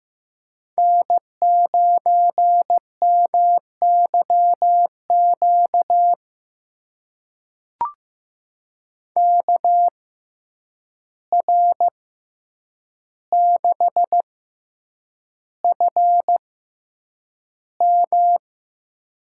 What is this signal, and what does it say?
Morse code, heard as a keyed tone, about 15 wpm, spaced as for 8 wpm, 700 Hz. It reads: N9MYQ KR6FM
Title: USA Call Signs - 5 Characters - 6x Character Spacing - 15wpm